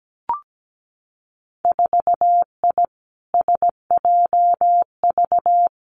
Morse code, heard as a keyed tone, about 17 wpm, 700 Hz. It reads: 4I SJV
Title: Winter Field Day Exchanges Encore 17wpm